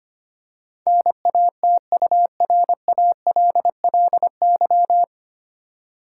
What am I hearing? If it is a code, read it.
Morse code, heard as a keyed tone, about 25 wpm, 700 Hz. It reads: NATURALLY
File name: practice / 25wpm / 9 Letter Words - First 3 Letters Given - 25wpm